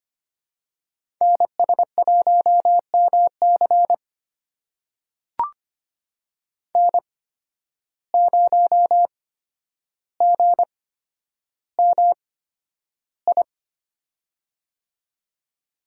Morse code, heard as a keyed tone, about 25 wpm, spaced as for 11 wpm, 700 Hz. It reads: NS1MC N0GMI
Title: USA Call Signs - 5 Characters - 8x Character Spacing - 25wpm